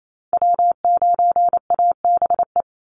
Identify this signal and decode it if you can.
Morse code, heard as a keyed tone, about 28 wpm, 700 Hz. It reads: W9ABE
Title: USA Call Signs 28wpm